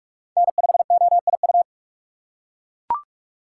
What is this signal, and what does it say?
Morse code, heard as a keyed tone, about 39 wpm, 700 Hz. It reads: N5OIU